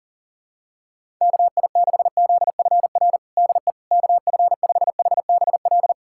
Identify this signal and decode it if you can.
Morse code, heard as a keyed tone, about 40 wpm, 700 Hz. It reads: KI6ZFR DE KF5HBL
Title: USA Callsign DE Callsign - 6 Characters - 40wpm